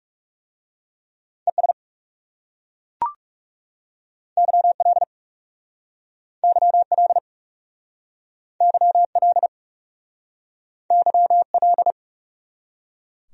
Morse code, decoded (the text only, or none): ES YL YL YL YL